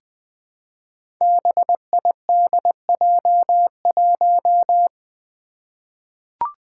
Morse code, heard as a keyed tone, about 20 wpm, 700 Hz. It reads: BIDJ1
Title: Five Challenging Letter-Number Combinations 20wpm